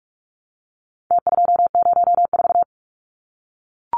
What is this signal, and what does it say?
Morse code, heard as a keyed tone, about 44 wpm, 700 Hz. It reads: T204
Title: Numbers - Contest Style - 4-Digit 45wpm